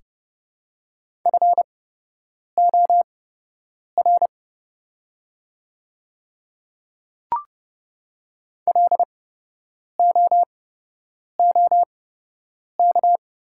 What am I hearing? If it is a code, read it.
Morse code, heard as a keyed tone, about 30 wpm, spaced as for 8 wpm, 700 Hz. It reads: FOR LOOK
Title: 8x Character Spacing - Top 100 Words - 30wpm